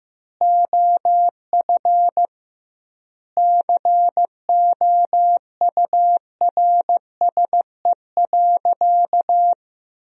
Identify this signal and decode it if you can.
Morse code, heard as a keyed tone, about 15 wpm, 700 Hz. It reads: OF COURSE.
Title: 2x Word-Spacing - Sentences from Top 500 Words - 15wpm